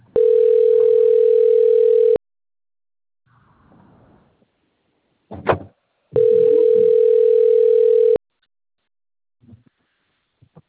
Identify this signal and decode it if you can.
Ringback tone (North America), 440 + 480 Hz.
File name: out- - - - - .11607